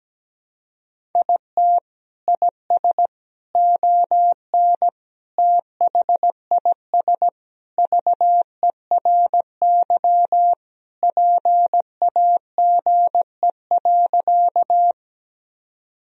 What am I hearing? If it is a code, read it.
Morse code, heard as a keyed tone, about 17 wpm, 700 Hz. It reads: IT IS ON THIS VERY PAGE.
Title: Sentences from Top 1000 Words - Review Part 1 - 17wpm